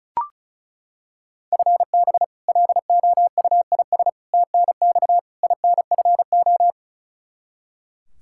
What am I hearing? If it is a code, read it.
Morse code, heard as a keyed tone, about 36 wpm, 700 Hz. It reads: FB LOUIS TNX INFO
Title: K1USN SST QSO Practice 35wpm